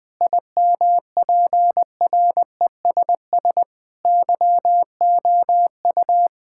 Morse code, heard as a keyed tone, about 20 wpm, 700 Hz. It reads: IMPRESS YOU
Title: Sentences from Top 1600 Words 20wpm